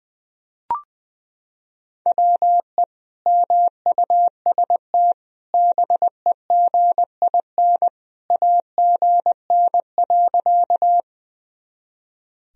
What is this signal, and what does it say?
Morse code, heard as a keyed tone, about 20 wpm, 700 Hz. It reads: WE MUST BEGIN AGN.
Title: Sentences from Top 300 Words and Abbreviations 20wpm